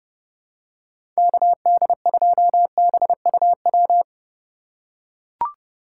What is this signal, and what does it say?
Morse code, heard as a keyed tone, about 30 wpm, 700 Hz. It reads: KD2BUW
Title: USA Call Signs - 6 Characters - 2x Character Spacing - 30wpm